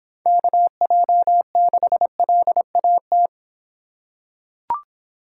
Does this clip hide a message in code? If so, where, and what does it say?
Yes: Morse code from 0.3 to 3.3 s, heard as a keyed tone, about 26 wpm, 700 Hz. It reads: KJ6LAT